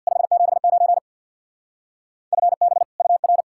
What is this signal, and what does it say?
Morse code, heard as a keyed tone, about 65 wpm, 700 Hz. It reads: 579 FB UR